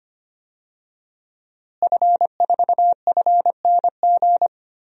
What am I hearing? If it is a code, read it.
Morse code, heard as a keyed tone, about 25 wpm, 700 Hz. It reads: F4FNG
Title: DXCC-100 Call Signs - 5 Characters - Repeated 2x - 25wpm